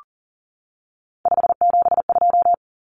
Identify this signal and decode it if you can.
Morse code, heard as a keyed tone, about 40 wpm, 700 Hz. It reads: 572
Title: Common Cut Numbers - 3-Digit 40wpm